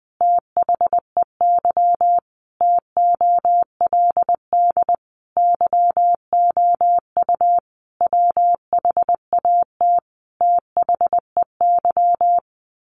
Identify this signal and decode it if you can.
Morse code, heard as a keyed tone, about 20 wpm, 700 Hz. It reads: THEY TOLD YOU WHAT THEY
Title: Sentences from Top 1000 Words 20wpm